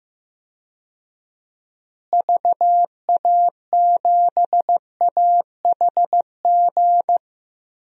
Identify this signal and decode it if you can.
Morse code, heard as a keyed tone, about 16 wpm, 700 Hz. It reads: VA7AHG
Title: DXCC-100 Call Signs - 6 Characters - Repeated 2x - 15wpm